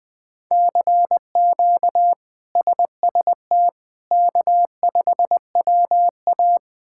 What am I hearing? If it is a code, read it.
Morse code, heard as a keyed tone, about 20 wpm, 700 Hz. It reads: CQ SST K5WA
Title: K1USN SST QSO Practice - 20wpm